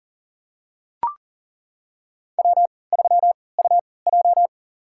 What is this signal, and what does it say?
Morse code, heard as a keyed tone, about 40 wpm, spaced as for 23 wpm, 700 Hz. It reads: W3UJ
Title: USA Call Signs - 4 Characters - 3x Character Spacing - 40wpm